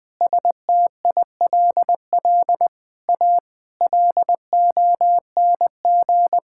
Morse code, heard as a keyed tone, about 20 wpm, 700 Hz. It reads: STILL A LONG
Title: Sets of 9 Words - 20wpm